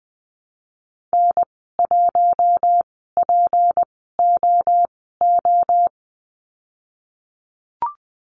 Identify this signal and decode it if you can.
Morse code, heard as a keyed tone, about 20 wpm, spaced as for 15 wpm, 700 Hz. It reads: N1POO
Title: 173 Lesson 28g - Number 9 - Call Signs 20wpm